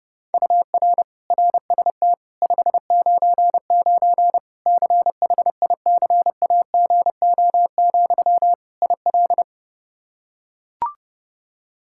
Morse code, heard as a keyed tone, about 30 wpm, 700 Hz. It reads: UR RST 599 CHICAGO, IL
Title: Practice QSO - Exchange 1 - Beginning - 30wpm